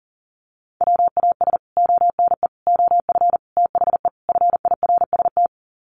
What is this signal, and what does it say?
Morse code, heard as a keyed tone, about 40 wpm, 700 Hz. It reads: WAS ONE OF THE FIRST
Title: Sets of 5 Words 40wpm